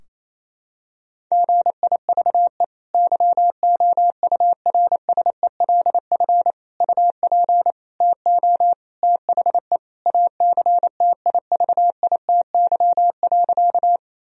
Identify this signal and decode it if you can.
Morse code, heard as a keyed tone, about 28 wpm, 700 Hz. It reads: GIVE YOURSELF UP TO THE ACTIVITY.
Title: Sentences from Top 1000 Words - Review Part 4 - 28wpm